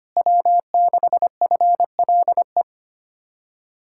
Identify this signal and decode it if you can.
Morse code, heard as a keyed tone, about 32 wpm, 700 Hz. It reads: W6FLE